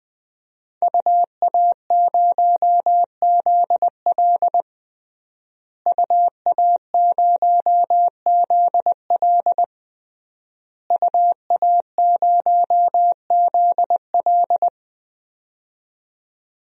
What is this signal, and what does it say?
Morse code, heard as a keyed tone, about 20 wpm, 700 Hz. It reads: UA0ZL UA0ZL UA0ZL
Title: DXCC-100 Call Signs - 5 Characters - Repeated 3x - 20wpm